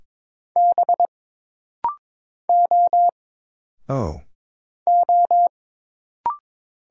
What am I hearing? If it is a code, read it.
Morse code, heard as a keyed tone, about 22 wpm, 700 Hz. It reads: B O O
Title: Single Letters - Rapid-Fire 22wpm